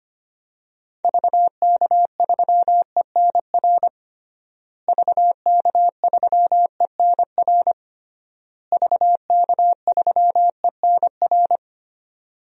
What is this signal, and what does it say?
Morse code, heard as a keyed tone, about 25 wpm, 700 Hz. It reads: VK3ENR VK3ENR VK3ENR
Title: DXCC-100 Call Signs - 6 Characters - Repeated 3x - 25wpm